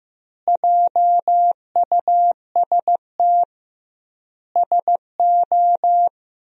Morse code, heard as a keyed tone, about 15 wpm, 700 Hz. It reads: JUST SO
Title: 2x Word-Spacing - Sentences from Top 200 Words - 15wpm